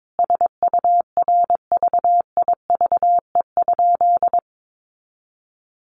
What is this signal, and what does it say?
Morse code, heard as a keyed tone, about 22 wpm, 700 Hz. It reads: SURVIVE?
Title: Sentences from Top 1300 Words 22wpm